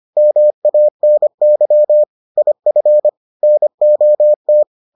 Morse code, heard as a keyed tone, about 25 wpm, 600 Hz. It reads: MANY IF NOT